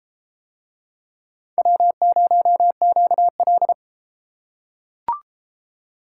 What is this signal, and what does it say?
Morse code, heard as a keyed tone, about 30 wpm, 700 Hz. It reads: W0QL